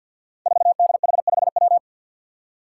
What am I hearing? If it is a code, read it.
Morse code, heard as a keyed tone, about 50 wpm, 700 Hz. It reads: 4DH5W
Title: Five Challenging Letter-Number Combinations 50wpm